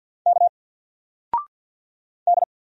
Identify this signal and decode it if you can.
Morse code, heard as a keyed tone, about 50 wpm, 700 Hz. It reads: K D
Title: Single Letter-Number 50wpm